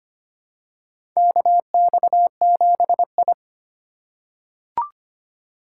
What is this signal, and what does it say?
Morse code, heard as a keyed tone, about 25 wpm, 700 Hz. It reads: KX7I
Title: USA Call Signs - 4 Characters - 3x Character Spacing - 25wpm